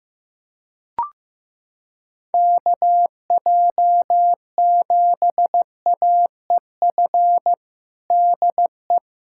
Morse code, heard as a keyed tone, about 15 wpm, 700 Hz. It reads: KJ7AEF DE
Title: USA Callsign DE Callsign 15wpm